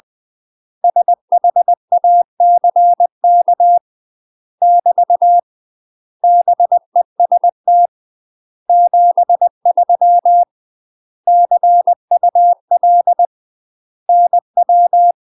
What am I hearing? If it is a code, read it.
Morse code, heard as a keyed tone, about 20 wpm, 700 Hz. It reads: SHACK <BT> BEST 73 CUL NW
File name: UK MASTER QSOs TRAINING / 20WPM / UK MASTER QSOs 0072